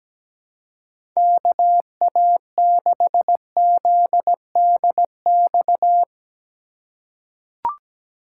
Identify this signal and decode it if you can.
Morse code, heard as a keyed tone, about 17 wpm, 700 Hz. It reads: KA6ZDX
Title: USA Call Signs - 6 Characters - 2x Character Spacing - 17wpm